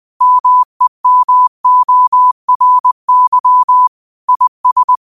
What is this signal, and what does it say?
Morse code, heard as a keyed tone, about 20 wpm, 1000 Hz. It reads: MEMORY IS